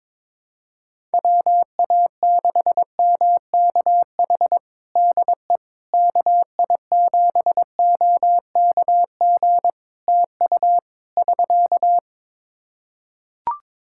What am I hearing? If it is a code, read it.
Morse code, heard as a keyed tone, about 21 wpm, 700 Hz. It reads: WA6MKH DE KI7OKG TU <SK>
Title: Practice QSO - Exchange 4 - Ending - 22wpm